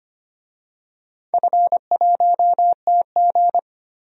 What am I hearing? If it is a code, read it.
Morse code, heard as a keyed tone, about 25 wpm, 700 Hz. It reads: F1TG
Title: DXCC-100 Call Signs - 4 Characters - 25wpm